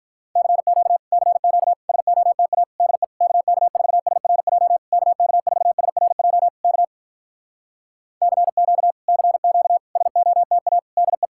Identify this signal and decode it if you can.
Morse code, heard as a keyed tone, about 46 wpm, 700 Hz. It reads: CQ CQ SOTA DE KK4SRJ KK4SRJ K CQ CQ SOTA DE